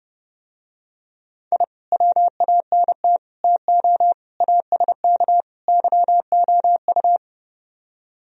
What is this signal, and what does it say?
Morse code, heard as a keyed tone, about 30 wpm, 700 Hz. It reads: I WANT TO ASK YOU